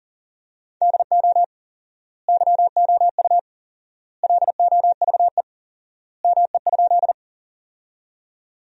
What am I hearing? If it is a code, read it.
Morse code, heard as a keyed tone, about 40 wpm, 700 Hz. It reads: DO YOU LOVE ME?
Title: 4x Word-Spacing - Sentences from Top 400 Words - 40wpm